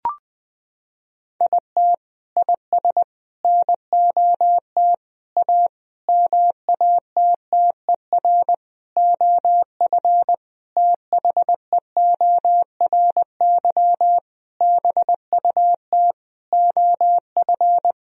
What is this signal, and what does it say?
Morse code, heard as a keyed tone, about 20 wpm, 700 Hz. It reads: IT IS NOT A MATTER OF THEORY BUT OF
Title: Sentences from Top 1000 Words - Review Part 4 - 20wpm